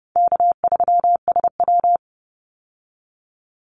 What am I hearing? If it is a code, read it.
Morse code, heard as a keyed tone, about 29 wpm, 700 Hz. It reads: K3SW
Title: Easy Call Signs Repeated 3x - 30wpm